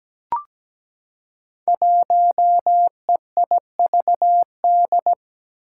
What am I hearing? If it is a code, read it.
Morse code, heard as a keyed tone, about 17 wpm, 700 Hz. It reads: 1EIVD
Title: Five Challenging Letter-Number Combinations 17wpm